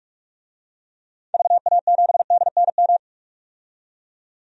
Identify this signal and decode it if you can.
Morse code, heard as a keyed tone, about 42 wpm, 700 Hz. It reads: VA7DNM
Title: DXCC-100 Call Signs - 6 Characters - 45wpm